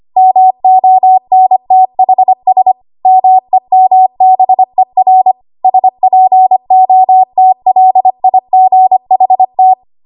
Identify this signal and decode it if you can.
Morse code, heard as a keyed tone, about 25 wpm, 750 Hz. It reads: MONTHS MEMBER SPOTLIGHT
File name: Archive / 25 WPM / 25WPM